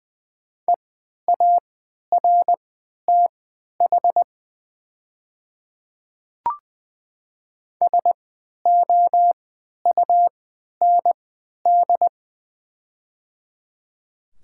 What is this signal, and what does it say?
Morse code, heard as a keyed tone, about 20 wpm, spaced as for 11 wpm, 690 Hz. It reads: EARTH SOUND
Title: 3x Character Spacing - Top 500 Words - 20wpm